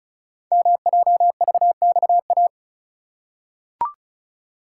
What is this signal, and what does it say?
Morse code, heard as a keyed tone, about 36 wpm, 700 Hz. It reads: MJVXA
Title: Code Groups - 5 Letters - 35wpm